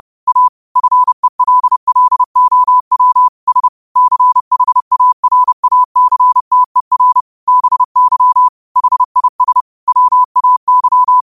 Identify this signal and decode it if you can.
Morse code, heard as a keyed tone, about 30 wpm, 1000 Hz. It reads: A FELLOW S CHARACTER BY HIS WAY